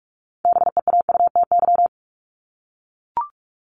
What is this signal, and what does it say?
Morse code, heard as a keyed tone, about 44 wpm, 700 Hz. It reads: BEAUTY
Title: Top 500 Words 45wpm